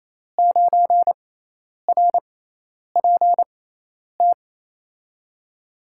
Morse code, heard as a keyed tone, about 28 wpm, spaced as for 10 wpm, 700 Hz. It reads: 9RPT
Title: USA Call Signs - 5 Characters - 6x Character Spacing - 28wpm